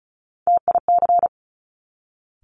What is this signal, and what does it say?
Morse code, heard as a keyed tone, about 36 wpm, 700 Hz. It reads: TIC
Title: Top Three Letter Combinations in Words 35wpm